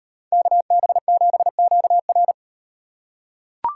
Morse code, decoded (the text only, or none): KB7QR